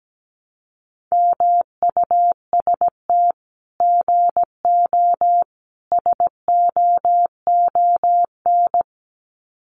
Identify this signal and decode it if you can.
Morse code, heard as a keyed tone, about 17 wpm, 700 Hz. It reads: MUST GO SOON
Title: Common QSO Phrases 17wpm